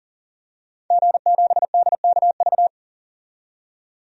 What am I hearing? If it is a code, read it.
Morse code, heard as a keyed tone, about 40 wpm, 700 Hz. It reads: G7DKV
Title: DXCC-100 Call Signs - 5 Characters - 40wpm